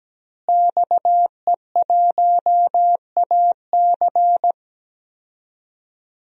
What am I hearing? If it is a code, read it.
Morse code, heard as a keyed tone, about 17 wpm, 700 Hz. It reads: XE1AC
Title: DXCC-100 Call Signs - 5 Characters - Repeated 3x - 17wpm